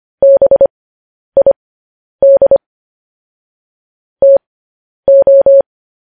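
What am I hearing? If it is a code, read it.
Morse code, heard as a keyed tone, about 25 wpm, spaced as for 10 wpm, 550 Hz. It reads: BID TO
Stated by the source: Morse code practice of news headlines at 10 wpm.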